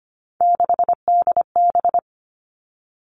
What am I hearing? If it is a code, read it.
Morse code, heard as a keyed tone, about 25 wpm, 700 Hz. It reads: 6DB
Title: Three Challenging Letter-Number Combinations 25wpm